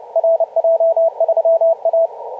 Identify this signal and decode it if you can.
Morse code, heard as a keyed tone, about 31 wpm, 660 Hz. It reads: RJ3A